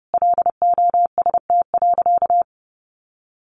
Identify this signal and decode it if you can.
Morse code, heard as a keyed tone, about 30 wpm, 700 Hz. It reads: LOST.
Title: Additional Easy Phrases 30wpm